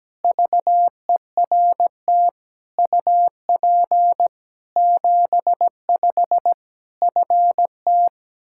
Morse code, heard as a keyed tone, about 17 wpm, 700 Hz. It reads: VERT UP 75 FT
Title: Practice QSO - Exchange 2 - Info - 17wpm